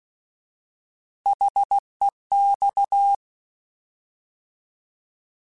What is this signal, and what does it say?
Morse code, heard as a keyed tone, about 16 wpm, 800 Hz. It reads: HEX
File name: Alphabetical / Alphabetical mp3s / Hs @15